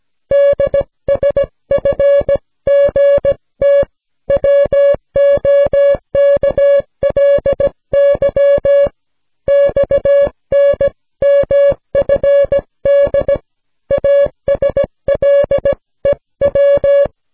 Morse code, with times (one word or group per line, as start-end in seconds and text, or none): 0.3-3.8 DSFGT
4.3-8.9 WOKLY
9.5-13.3 XNMFD
13.9-17.0 ASLEW